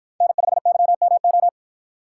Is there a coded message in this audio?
Yes: Morse code from 0.2 to 1.5 s, heard as a keyed tone, about 53 wpm, 700 Hz. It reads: N5YMO